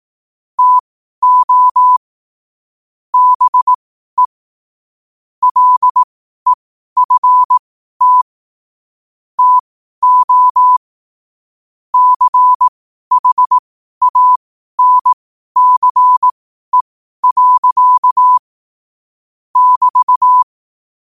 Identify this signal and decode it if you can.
Morse code, heard as a keyed tone, about 18 wpm, spaced as for 12 wpm, 1000 Hz. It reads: TO BE LEFT TO CHANCE. <BT>
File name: qotdfiles / 12wpm